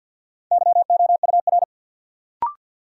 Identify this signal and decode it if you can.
Morse code, heard as a keyed tone, about 54 wpm, 700 Hz. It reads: YOUR
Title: Sentences from Top 100 Words - New Word Drill - 50wpm